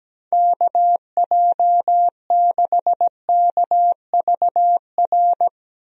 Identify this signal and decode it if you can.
Morse code, heard as a keyed tone, about 17 wpm, 700 Hz. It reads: KJ6KVR
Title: USA Call Signs - 6 Characters - Repeated 2x - 17wpm